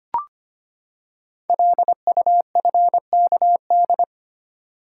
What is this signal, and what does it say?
Morse code, heard as a keyed tone, about 25 wpm, 700 Hz. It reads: LUFKD
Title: Code Groups - 5 Letters - 25wpm